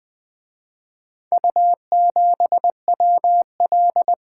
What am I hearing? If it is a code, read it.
Morse code, heard as a keyed tone, about 23 wpm, 700 Hz. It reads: U7WL
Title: Speed-Racing - Four Letter-Number Combinations - 20wpm